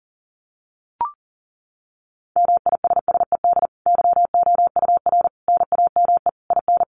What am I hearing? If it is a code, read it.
Morse code, heard as a keyed tone, about 40 wpm, 700 Hz. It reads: MISSED YOUR NAME IN
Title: Common QSO Phrases 40wpm